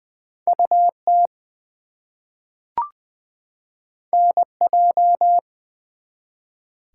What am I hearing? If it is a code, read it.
Morse code, heard as a keyed tone, about 20 wpm, 700 Hz. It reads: UT NJ
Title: US States and Canadian Provinces 20wpm